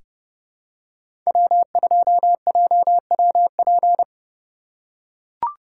Morse code, heard as a keyed tone, about 29 wpm, 700 Hz. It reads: W2JWP
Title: USA Call Signs - 5 Characters - 4x Character Spacing - 30wpm